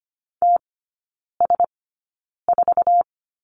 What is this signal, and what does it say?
Morse code, heard as a keyed tone, about 25 wpm, spaced as for 9 wpm, 700 Hz. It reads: TS4
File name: course / 25:10wpm / 093 Lesson 16e - Letter c - Three Characters 25wpm